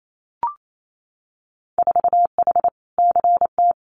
Morse code, heard as a keyed tone, about 28 wpm, 700 Hz. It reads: 4H CT
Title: Winter Field Day Exchanges Encore 28wpm